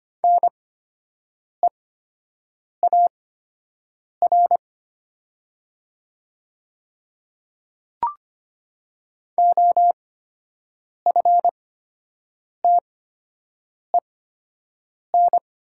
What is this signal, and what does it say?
Morse code, heard as a keyed tone, about 25 wpm, spaced as for 7 wpm, 700 Hz. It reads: NEAR OFTEN
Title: 8x Character Spacing - Top 300 Words - 25wpm